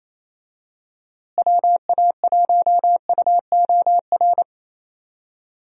Morse code, heard as a keyed tone, about 27 wpm, 700 Hz. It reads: WA1UOR
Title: USA Call Signs - 6 Characters - Repeated 3x - 28wpm